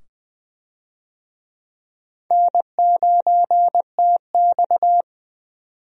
Morse code, heard as a keyed tone, about 20 wpm, 700 Hz. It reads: N9TX
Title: USA Call Signs - 4 Characters - 20wpm